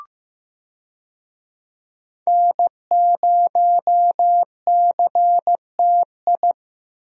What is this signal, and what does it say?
Morse code, heard as a keyed tone, about 14 wpm, 700 Hz. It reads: N0CTI
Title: USA Call Signs - 5 Characters - Repeated 3x - 15wpm